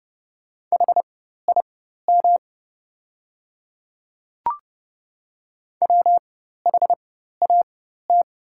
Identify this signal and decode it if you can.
Morse code, heard as a keyed tone, about 30 wpm, spaced as for 14 wpm, 700 Hz. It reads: HIM WHAT
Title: 4x Character Spacing - Top 100 Words - 30wpm